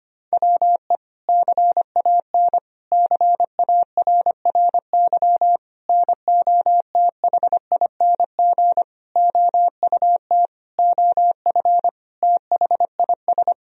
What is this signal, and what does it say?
Morse code, heard as a keyed tone, about 25 wpm, 700 Hz. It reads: WE CAN CARRY NOTHING OUT OF THIS
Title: Sentences from Top 1000 Words - Review Part 1 - 25wpm